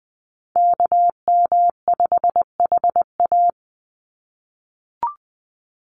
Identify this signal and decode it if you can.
Morse code, heard as a keyed tone, about 20 wpm, 700 Hz. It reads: KM5HA
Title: USA Call Signs 20wpm